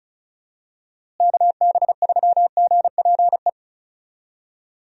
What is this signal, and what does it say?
Morse code, heard as a keyed tone, about 36 wpm, 700 Hz. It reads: KB3GPE
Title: USA Call Signs - 6 Characters - Repeated 2x - 35wpm